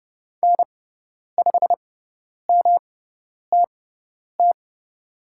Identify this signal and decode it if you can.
Morse code, heard as a keyed tone, about 30 wpm, spaced as for 10 wpm, 700 Hz. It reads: N5MTT
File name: Lesson 18g - Letter m - Call Signs 30wpm